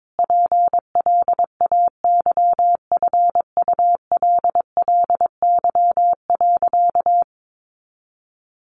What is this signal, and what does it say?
Morse code, heard as a keyed tone, about 22 wpm, 700 Hz. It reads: PLAYFULLY.
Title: A Sound Of Thunder 22wpm